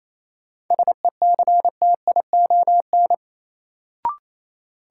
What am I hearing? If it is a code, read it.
Morse code, heard as a keyed tone, about 28 wpm, 700 Hz. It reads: SECTION
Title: Sentences from Top 1000 Words - Review Part 5 - New Word Drill - 28wpm